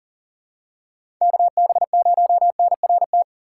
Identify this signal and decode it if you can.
Morse code, heard as a keyed tone, about 40 wpm, 700 Hz. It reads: KB0NRT
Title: USA Call Signs - 6x Character Spacing - 40wpm